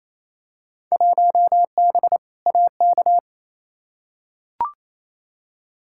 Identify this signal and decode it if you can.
Morse code, heard as a keyed tone, about 28 wpm, 700 Hz. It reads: 1B AK
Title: Field Day Exchanges Encore - 28wpm